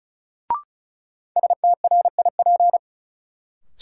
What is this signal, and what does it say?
Morse code, heard as a keyed tone, about 36 wpm, 700 Hz. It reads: STRIP
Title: Sentences from Top 1600 Words - New Word Drill Alt - 35wpm